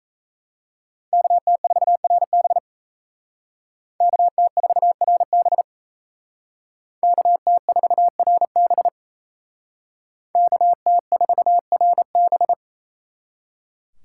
Morse code, heard as a keyed tone, about 35 wpm, 700 Hz. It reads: KT4RB KT4RB KT4RB KT4RB